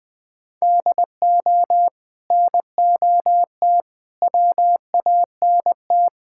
Morse code, heard as a keyed tone, about 21 wpm, 700 Hz. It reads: DO NOT WANT
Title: Build A Phrase - 5-Word Phrases - 20wpm